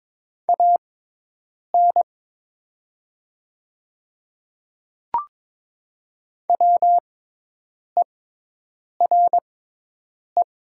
Morse code, heard as a keyed tone, about 22 wpm, spaced as for 8 wpm, 700 Hz. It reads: AN WERE
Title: 6x Character Spacing - Top 100 Words - 22wpm